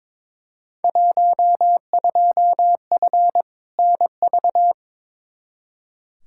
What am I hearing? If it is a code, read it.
Morse code, heard as a keyed tone, about 22 wpm, 700 Hz. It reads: 12F NV
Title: Field Day Exchanges Encore - 22wpm